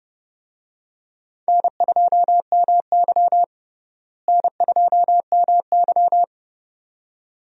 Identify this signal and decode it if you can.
Morse code, heard as a keyed tone, about 30 wpm, 700 Hz. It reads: N2MY N2MY
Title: USA Call Signs - 4 Characters - Repeated 2x - 30wpm